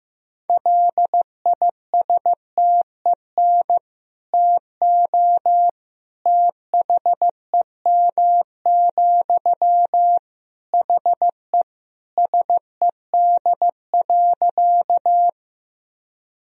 Morse code, heard as a keyed tone, about 15 wpm, 700 Hz. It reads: LISTEN TO THEM, HE SED.
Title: Sentences from Top 400 Words and Abbreviations 15wpm